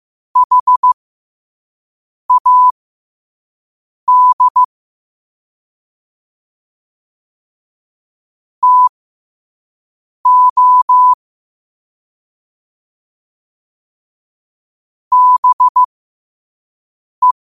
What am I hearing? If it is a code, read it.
Morse code, heard as a keyed tone, about 15 wpm, spaced as for 5 wpm, 1000 Hz. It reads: HAD TO BE
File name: qotdfiles / 5wpm